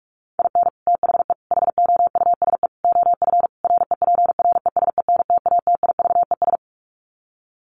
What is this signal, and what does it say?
Morse code, heard as a keyed tone, about 44 wpm, 700 Hz. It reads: IN THE HOUSE OF REPRESENTATIVES
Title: Sets of 5 Words Encore 45wpm